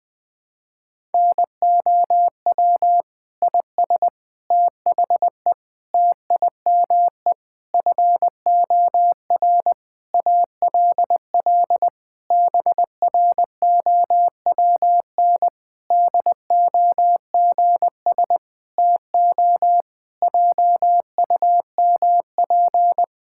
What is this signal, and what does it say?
Morse code, heard as a keyed tone, about 20 wpm, 700 Hz. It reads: NOW IS THE TIME FOR ALL BROWN DOGS TO JUMP
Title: Pangram Challenge v2 - 20wpm